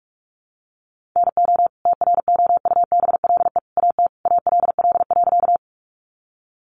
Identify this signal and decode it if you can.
Morse code, heard as a keyed tone, about 44 wpm, 700 Hz. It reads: NO TROUBLE AT ALL.
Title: Sentences from Top 1200 Words 45wpm